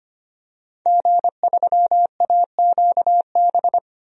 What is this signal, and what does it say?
Morse code, heard as a keyed tone, about 25 wpm, 700 Hz. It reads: G3AQB
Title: DXCC-100 Call Signs - 5 Characters - Repeated 2x - 25wpm